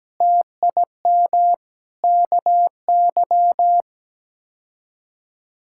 Morse code, heard as a keyed tone, about 17 wpm, 700 Hz. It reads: TIM KY
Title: K1USN SST QSO Practice - 17wpm